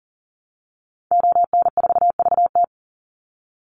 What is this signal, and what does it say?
Morse code, heard as a keyed tone, about 40 wpm, 700 Hz. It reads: ON4VT
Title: Call Signs - 40wpm